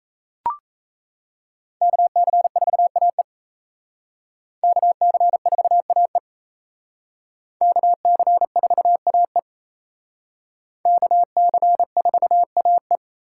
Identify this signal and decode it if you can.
Morse code, heard as a keyed tone, about 36 wpm, 700 Hz. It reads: KC4AE KC4AE KC4AE KC4AE